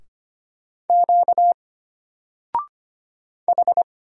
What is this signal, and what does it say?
Morse code, heard as a keyed tone, about 25 wpm, 700 Hz. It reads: Q H
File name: Single-Letter-Number-25wpm